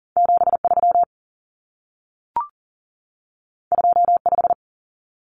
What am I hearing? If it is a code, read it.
Morse code, heard as a keyed tone, about 40 wpm, 700 Hz. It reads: 73 25
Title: Numbers - 2-Digit 40wpm